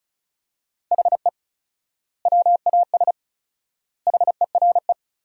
Morse code, heard as a keyed tone, about 36 wpm, 700 Hz. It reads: HE WAS HERE
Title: 4x Word-Spacing - Sentences from Top 500 Words - 35wpm